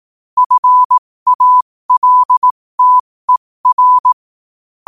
Morse code, heard as a keyed tone, about 18 wpm, spaced as for 15 wpm, 1000 Hz. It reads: FALTER
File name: qotdfiles / 15wpm